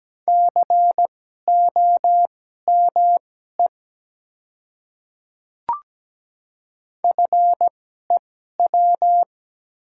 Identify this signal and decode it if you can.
Morse code, heard as a keyed tone, about 17 wpm, spaced as for 12 wpm, 700 Hz. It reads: COME FEW
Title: 2x Character Spacing - Top 300 Words - 17wpm